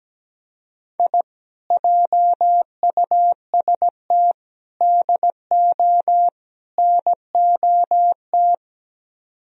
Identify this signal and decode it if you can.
Morse code, heard as a keyed tone, about 17 wpm, 700 Hz. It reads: I JUST DO NOT
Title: Speed-Racing - Sets of 4 Words 17wpm